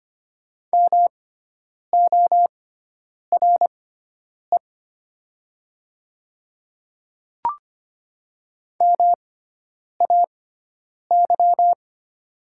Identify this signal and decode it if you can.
Morse code, heard as a keyed tone, about 25 wpm, spaced as for 9 wpm, 700 Hz. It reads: MORE MAY
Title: 6x Character Spacing - Top 100 Words - 25wpm